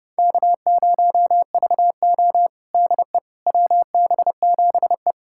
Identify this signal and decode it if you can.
Morse code, heard as a keyed tone, about 30 wpm, 700 Hz. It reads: K0VO DE WB7E
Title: USA Callsign DE Callsign - 4 Characters - 30wpm